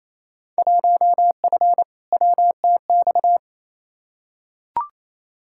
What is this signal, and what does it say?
Morse code, heard as a keyed tone, about 28 wpm, 700 Hz. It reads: 1F WTX
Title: Field Day Exchanges Encore - 28wpm